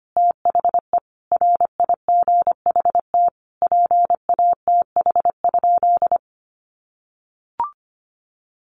Morse code, heard as a keyed tone, about 25 wpm, 700 Hz. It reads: THE RIGHT PATH?
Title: Sentences from Top 1400 Words 25wpm